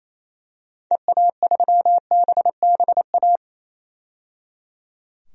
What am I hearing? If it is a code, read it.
Morse code, heard as a keyed tone, about 28 wpm, 700 Hz. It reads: EA3BBA
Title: DXCC-100 Call Signs - 6 Characters - 28wpm